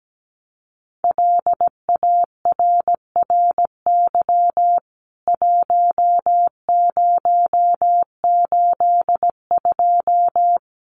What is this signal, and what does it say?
Morse code, heard as a keyed tone, about 17 wpm, 700 Hz. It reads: LARRY 1082
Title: CWT Exchange 17wpm